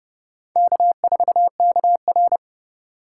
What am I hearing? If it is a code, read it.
Morse code, heard as a keyed tone, about 26 wpm, 700 Hz. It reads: K4KR